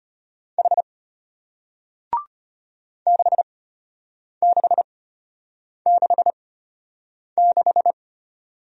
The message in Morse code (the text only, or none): H 6666